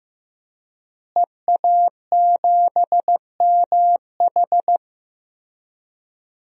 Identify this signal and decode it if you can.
Morse code, heard as a keyed tone, about 15 wpm, 700 Hz. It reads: EA7MH
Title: DXCC-100 Call Signs - 5 Characters - Repeated 2x - 15wpm